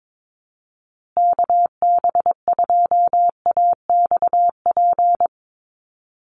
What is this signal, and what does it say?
Morse code, heard as a keyed tone, about 22 wpm, 700 Hz. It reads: KB2AXP
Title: USA Call Signs Repeated 2x 22wpm